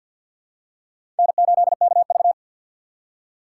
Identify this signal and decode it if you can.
Morse code, heard as a keyed tone, about 51 wpm, 700 Hz. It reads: N8KV
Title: USA Call Signs - 4 Characters - 2x Character Spacing - 50wpm